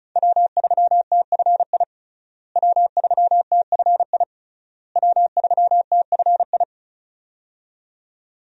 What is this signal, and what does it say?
Morse code, heard as a keyed tone, about 36 wpm, 700 Hz. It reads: W3TFI W3TFI W3TFI
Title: USA Call Signs - 5 Characters - Repeated 3x - 35wpm